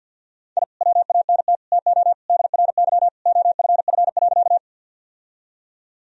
Morse code, heard as a keyed tone, about 50 wpm, 700 Hz. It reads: I WANT TO DRY OFF.
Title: Sentences from Top 1000 Words - Review Part 3 - 50wpm